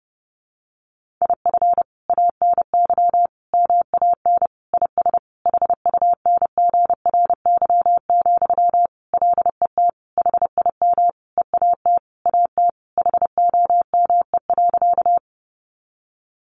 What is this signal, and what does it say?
Morse code, heard as a keyed tone, about 30 wpm, 700 Hz. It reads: IF ANY MAN IS HUNGRY, LET HIM EAT AT HOME.
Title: Sentences from Top 1400 Words 30wpm